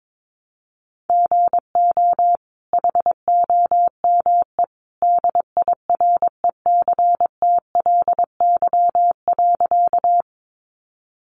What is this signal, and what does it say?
Morse code, heard as a keyed tone, about 22 wpm, 700 Hz. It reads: GO HOME DIRECTLY.
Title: Sentences from Top 800 Words 22wpm